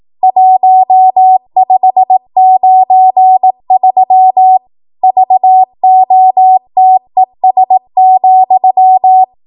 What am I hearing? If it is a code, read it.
Morse code, heard as a keyed tone, about 18 wpm, 750 Hz. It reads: 1593 VOTES,